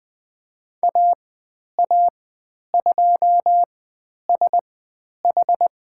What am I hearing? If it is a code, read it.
Morse code, heard as a keyed tone, about 20 wpm, spaced as for 10 wpm, 700 Hz. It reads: AA2SH
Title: 109 Lesson 18g - Letter m - Call Signs 20wpm